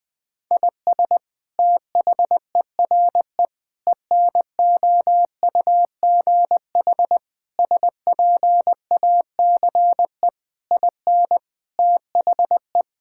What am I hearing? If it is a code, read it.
Morse code, heard as a keyed tone, about 20 wpm, 700 Hz. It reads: IS THERE ENOUGH SPACE IN THE
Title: Sentences from Top 1000 Words - Review Part 5 - 20wpm